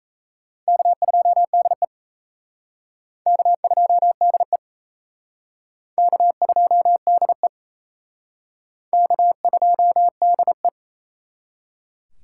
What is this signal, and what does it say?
Morse code, heard as a keyed tone, about 34 wpm, 700 Hz. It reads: K2DE K2DE K2DE K2DE
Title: Speed Racing - USA 1x2 Call Signs 28wpm